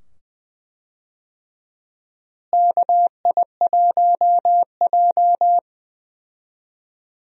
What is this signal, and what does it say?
Morse code, heard as a keyed tone, about 20 wpm, 700 Hz. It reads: KI1J
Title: USA Call Signs - 4 Characters - Repeated 2x - 20wpm